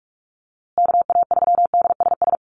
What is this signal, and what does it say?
Morse code, heard as a keyed tone, about 44 wpm, 700 Hz. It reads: KA3DSS
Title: USA Call Signs Repeated 3x 45wpm